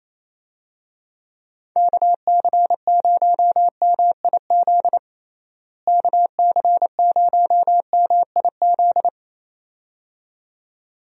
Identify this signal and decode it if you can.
Morse code, heard as a keyed tone, about 28 wpm, 700 Hz. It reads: KC0MIZ KC0MIZ
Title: USA Call Signs - 6 Characters - Repeated 2x - 28wpm